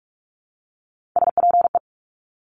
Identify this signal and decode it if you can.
Morse code, heard as a keyed tone, about 46 wpm, 700 Hz. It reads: SPE
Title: Top Three Letter Combinations in Words 45wpm